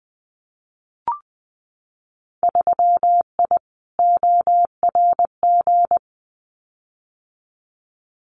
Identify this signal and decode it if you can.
Morse code, heard as a keyed tone, about 20 wpm, 700 Hz. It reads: 3I ORG
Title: Winter Field Day Exchanges Encore 20wpm